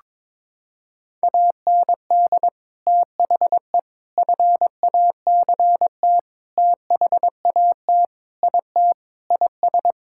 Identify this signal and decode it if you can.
Morse code, heard as a keyed tone, about 22 wpm, 700 Hz. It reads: AND THE FACT THAT IT IS
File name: practice / 22wpm / Sentences with 4-Word Phrases - 22wpm